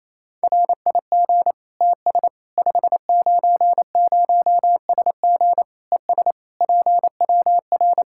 Morse code, heard as a keyed tone, about 28 wpm, 700 Hz. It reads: RIG TS 590SG ES PWR
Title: Practice QSO - Exchange 2 - Info - 28wpm